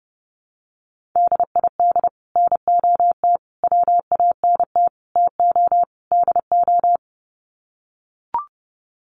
Morse code, heard as a keyed tone, about 30 wpm, 700 Hz. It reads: DID NOT WANT TO DO
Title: Sets of 5 Words Encore 30wpm